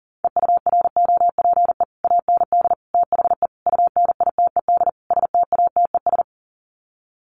Morse code, heard as a keyed tone, about 40 wpm, 700 Hz. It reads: EUROPE AND THE UNITED STATES
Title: Sets of 5 Words Encore2 40wpm